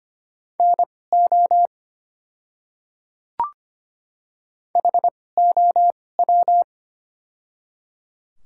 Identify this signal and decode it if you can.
Morse code, heard as a keyed tone, about 25 wpm, spaced as for 18 wpm, 700 Hz. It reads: NO HOW
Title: 2x Character Spacing - Top 100 Words - 25wpm